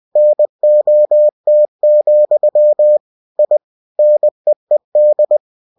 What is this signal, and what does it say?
Morse code, heard as a keyed tone, about 20 wpm, 600 Hz. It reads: NOT, I NEED
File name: cw-books / winnie-the-pooh / winnie-20wpm-page-0107